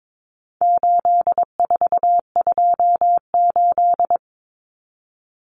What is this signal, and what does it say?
Morse code, heard as a keyed tone, about 22 wpm, 700 Hz. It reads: 8428
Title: Numbers - 4-Digit 22wpm